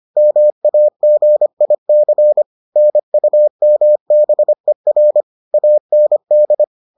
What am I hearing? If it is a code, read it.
Morse code, heard as a keyed tone, about 25 wpm, 600 Hz. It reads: MAGIC NUMBER AND